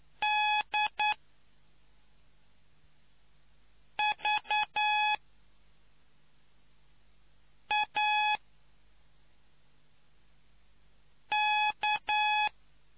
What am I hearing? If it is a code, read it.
Morse code, heard as a keyed tone, about 9 wpm, 840 Hz. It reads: D V A K